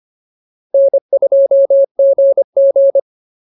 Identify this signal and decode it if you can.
Morse code, heard as a keyed tone, about 25 wpm, 590 Hz. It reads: N2GG